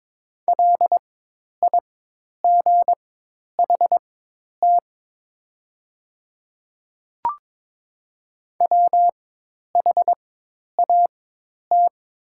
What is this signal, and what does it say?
Morse code, heard as a keyed tone, about 22 wpm, spaced as for 10 wpm, 700 Hz. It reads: LIGHT WHAT
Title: 4x Character Spacing - Top 300 Words - 22wpm